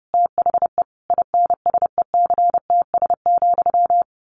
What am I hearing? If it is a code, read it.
Morse code, heard as a keyed tone, about 30 wpm, 700 Hz. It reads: THE INSECTS,
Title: A Sound Of Thunder 30wpm